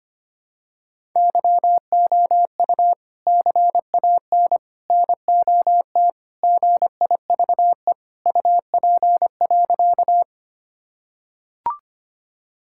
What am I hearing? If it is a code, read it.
Morse code, heard as a keyed tone, about 25 wpm, 700 Hz. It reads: YOU CAN NOT GIVE UP.
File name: practice / 25wpm / Sentences with 3-Word Phrases - 25wpm